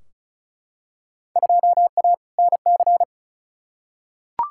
Morse code, heard as a keyed tone, about 36 wpm, 700 Hz. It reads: 2A NC
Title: Field Day Exchanges - 35wpm